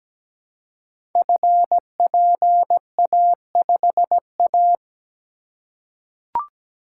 Morse code, heard as a keyed tone, about 17 wpm, 700 Hz. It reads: FPA5A
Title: Code Groups - 5 Letter:Numbers - 17wpm